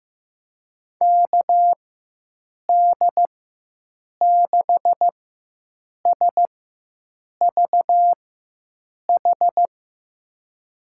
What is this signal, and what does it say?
Morse code, heard as a keyed tone, about 15 wpm, spaced as for 7 wpm, 700 Hz. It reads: KD6SVH
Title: USA Call Signs - 4x Character Spacing - 15wpm